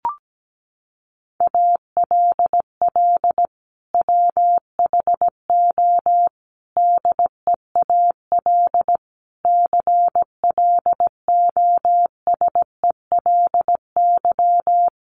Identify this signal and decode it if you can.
Morse code, heard as a keyed tone, about 17 wpm, 700 Hz. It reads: ALL WHO DEAL CLOSELY
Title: Sentences from Top 1400 Words 17wpm